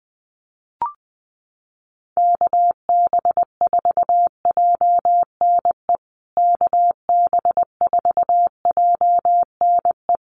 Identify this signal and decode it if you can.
Morse code, heard as a keyed tone, about 20 wpm, 700 Hz. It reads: KB4JNE KB4JNE
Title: USA Call Signs Repeated 3x 20wpm